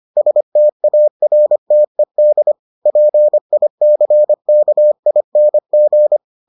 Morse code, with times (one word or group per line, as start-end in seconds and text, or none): 0.2-2.5 STARTED
2.9-6.2 PICKING